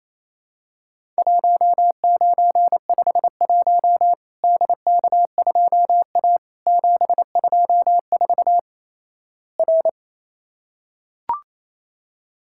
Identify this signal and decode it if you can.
Morse code, heard as a keyed tone, about 27 wpm, 700 Hz. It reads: 1951 DK2A 724 R
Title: Worked All Europe DX Contest - QTC Exchange - 28wpm